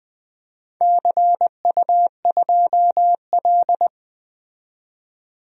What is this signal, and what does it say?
Morse code, heard as a keyed tone, about 20 wpm, 700 Hz. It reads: CU2L
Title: DXCC-100 Call Signs - 4 Characters - 20wpm